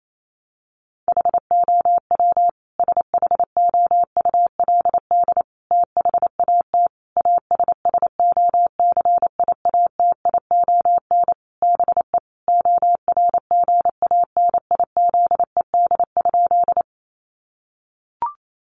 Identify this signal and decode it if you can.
Morse code, heard as a keyed tone, about 28 wpm, 700 Hz. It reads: HOW SHOULD THAT ASSOCIATION BE ORGANIZED?
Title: Sentences from Top 1300 Words 28wpm